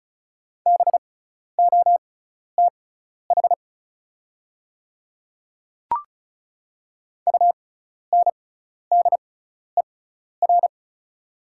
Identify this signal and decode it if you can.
Morse code, heard as a keyed tone, about 36 wpm, spaced as for 12 wpm, 690 Hz. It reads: BOTH UNDER
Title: 6x Character Spacing - Top 300 Words - 35wpm